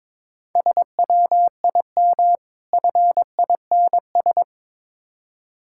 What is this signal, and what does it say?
Morse code, heard as a keyed tone, about 24 wpm, 690 Hz. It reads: SWIM FINS
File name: Easy Words - Speed-Racing with Hints - 22wpm